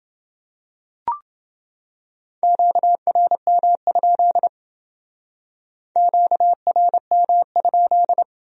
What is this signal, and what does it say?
Morse code, heard as a keyed tone, about 29 wpm, 700 Hz. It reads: QRM? QRM?